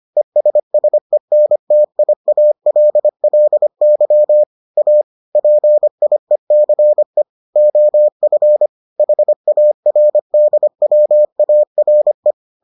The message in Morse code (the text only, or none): ESSENTIALLY A PIECE OF HARDWARE